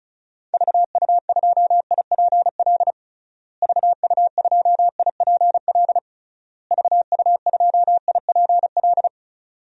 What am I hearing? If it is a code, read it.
Morse code, heard as a keyed tone, about 36 wpm, 700 Hz. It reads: VU2IPL VU2IPL VU2IPL
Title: DXCC-100 Call Signs - 6 Characters - Repeated 3x - 35wpm